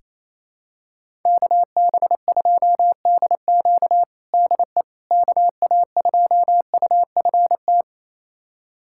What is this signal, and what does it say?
Morse code, heard as a keyed tone, about 27 wpm, 700 Hz. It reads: KB2DQ DE KA2UFT
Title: USA Callsign DE Callsign 28wpm